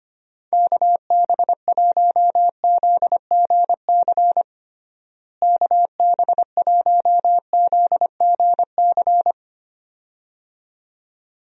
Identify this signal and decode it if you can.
Morse code, heard as a keyed tone, about 25 wpm, 700 Hz. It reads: KB1ZGC KB1ZGC
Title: USA Call Signs - 6 Characters - Repeated 2x - 25wpm